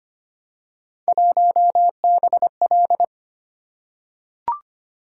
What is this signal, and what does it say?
Morse code, heard as a keyed tone, about 25 wpm, 700 Hz. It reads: 1BL
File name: Three Letter-Number Combinations - 25wpm